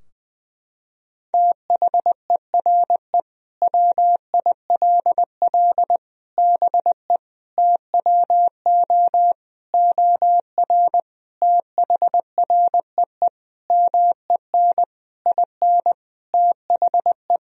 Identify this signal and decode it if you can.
Morse code, heard as a keyed tone, about 20 wpm, 700 Hz. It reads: THERE WILL BE TWO OR THREE MEN IN THE
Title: Sentences from Top 1000 Words - Review Part 3 - 20wpm